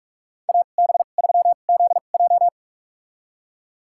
Morse code, heard as a keyed tone, about 48 wpm, spaced as for 33 wpm, 690 Hz. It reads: AB3ZJ
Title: USA Call Signs - 5 Characters - 2x Character Spacing - 45wpm